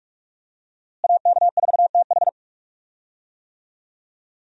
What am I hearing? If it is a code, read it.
Morse code, heard as a keyed tone, about 46 wpm, 700 Hz. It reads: AK4TH
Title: USA Call Signs - 5 Characters - Repeated 3x - 45wpm